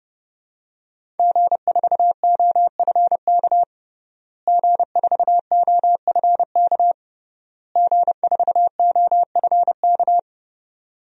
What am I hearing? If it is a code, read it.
Morse code, heard as a keyed tone, about 30 wpm, 700 Hz. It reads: G4OFK G4OFK G4OFK
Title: DXCC-100 Call Signs - 5 Characters - Repeated 3x - 30wpm